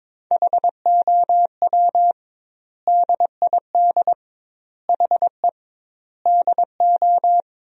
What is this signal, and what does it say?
Morse code, heard as a keyed tone, about 23 wpm, 700 Hz. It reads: HOW DID HE DO
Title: 2x Word-Spacing - Sentences from Top 100 Words - 22wpm